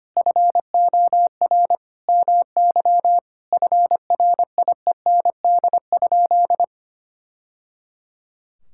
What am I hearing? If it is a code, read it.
Morse code, heard as a keyed tone, about 25 wpm, 700 Hz. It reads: FOR MY FRIEND?
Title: Sentences from Top 1000 Words - Review Part 5 - 25wpm